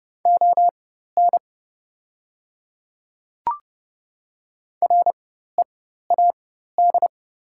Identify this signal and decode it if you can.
Morse code, heard as a keyed tone, about 30 wpm, spaced as for 14 wpm, 700 Hz. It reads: ON READ
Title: 4x Character Spacing - Top 200 Words - 30wpm